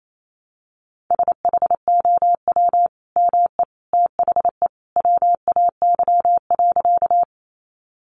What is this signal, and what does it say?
Morse code, heard as a keyed tone, about 28 wpm, 700 Hz. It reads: SHOW ME THE WAY.
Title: Sentences from Top 200 Words 28wpm